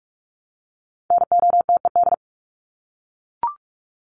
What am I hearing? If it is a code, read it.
Morse code, heard as a keyed tone, about 44 wpm, 700 Hz. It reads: NOTED
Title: ED Suffix - Top 100 Words with Base Words 45wpm